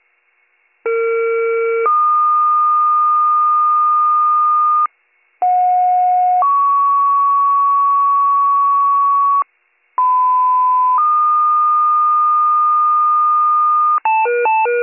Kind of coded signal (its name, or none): two-tone page